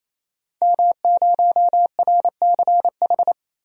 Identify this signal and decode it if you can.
Morse code, heard as a keyed tone, about 28 wpm, 700 Hz. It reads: M0RCH
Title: DXCC-100 Call Signs - 5 Characters - 28wpm